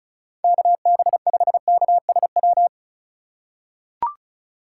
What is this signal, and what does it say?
Morse code, heard as a keyed tone, about 34 wpm, 700 Hz. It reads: KB5KSW
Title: USA Call Signs - 2x Character Spacing - 35wpm